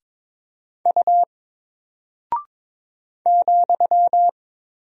Morse code, heard as a keyed tone, about 22 wpm, 700 Hz. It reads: U ,
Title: Single Character (Letter:Number:Punctuation:Prosign) Alt 22wpm